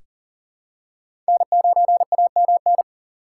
Morse code, heard as a keyed tone, about 40 wpm, 700 Hz. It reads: N9AMN
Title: USA Call Signs - 5 Characters - 4x Character Spacing - 40wpm